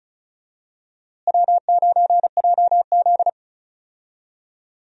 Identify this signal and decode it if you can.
Morse code, heard as a keyed tone, about 36 wpm, 700 Hz. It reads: W9JZ
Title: USA Call Signs - 4 Characters - Repeated 2x - 35wpm